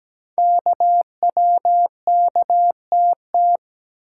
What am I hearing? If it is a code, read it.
Morse code, heard as a keyed tone, about 17 wpm, 700 Hz. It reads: KWKTT
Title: Code Groups - 5 Letters - 17wpm